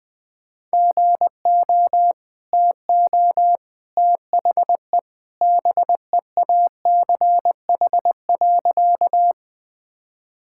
Morse code, heard as a keyed tone, about 20 wpm, 700 Hz. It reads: GO TO THE BEACH.
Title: Easy Sentences with Hints - 20wpm